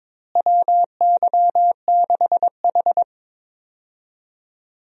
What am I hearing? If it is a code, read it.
Morse code, heard as a keyed tone, about 22 wpm, 700 Hz. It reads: WY6H
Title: USA Call Signs - 4 Characters - Repeated 3x - 22wpm